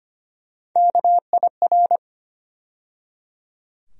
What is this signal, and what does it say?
Morse code, heard as a keyed tone, about 25 wpm, 700 Hz. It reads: KIR
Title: USA Call Sign Postfixes - 3 Letters 25wpm